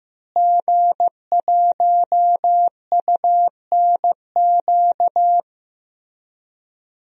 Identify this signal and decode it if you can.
Morse code, heard as a keyed tone, about 15 wpm, 700 Hz. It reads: G1UNQ
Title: DXCC-100 Call Signs - 5 Characters - Repeated 3x - 15wpm